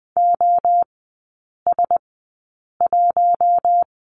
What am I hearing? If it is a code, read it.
Morse code, heard as a keyed tone, about 20 wpm, spaced as for 8 wpm, 700 Hz. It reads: OS1
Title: 041 Lesson 8e - Number 1 - Three Characters 20wpm